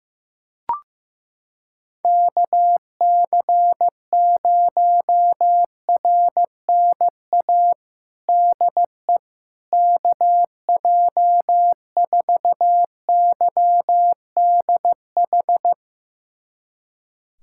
Morse code, heard as a keyed tone, about 15 wpm, 700 Hz. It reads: KC0RNA DE KJ4YDH
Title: USA Callsign DE Callsign 15wpm